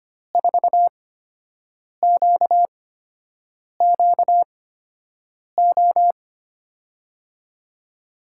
Morse code, heard as a keyed tone, about 25 wpm, spaced as for 7 wpm, 700 Hz. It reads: 4QQO
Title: USA Call Signs - 6 Characters - 8x Character Spacing - 25wpm